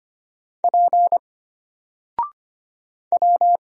Morse code, heard as a keyed tone, about 25 wpm, 700 Hz. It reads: P W
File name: Single-Letter-Number-25wpm